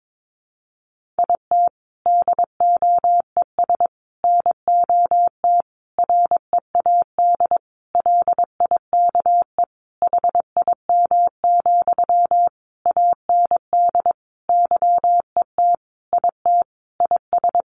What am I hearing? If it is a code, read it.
Morse code, heard as a keyed tone, about 22 wpm, 700 Hz. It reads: IT DOES NOT READ LIKE HIM, AND YET IT IS
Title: Sentences from Top 800 Words 22wpm